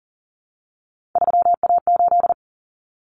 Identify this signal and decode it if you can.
Morse code, heard as a keyed tone, about 40 wpm, 700 Hz. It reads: 3A8
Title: Common Cut Numbers - 3-Digit 40wpm